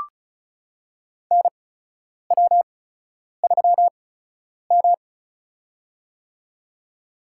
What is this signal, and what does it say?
Morse code, heard as a keyed tone, about 36 wpm, spaced as for 10 wpm, 700 Hz. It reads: NW3M
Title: USA Call Signs - 4 Characters - 8x Character Spacing - 35wpm